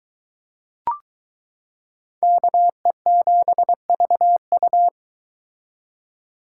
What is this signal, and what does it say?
Morse code, heard as a keyed tone, about 20 wpm, 700 Hz. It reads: KE7VU